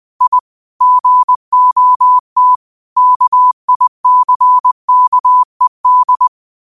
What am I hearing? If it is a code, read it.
Morse code, heard as a keyed tone, about 20 wpm, 1000 Hz. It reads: I GOT KICKED
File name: qotdfiles / 20wpm